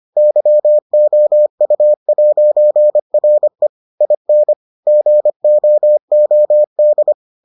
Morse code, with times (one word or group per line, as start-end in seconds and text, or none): 0.2-3.7 YOU'RE
4.0-4.5 IN
4.9-7.1 GOOD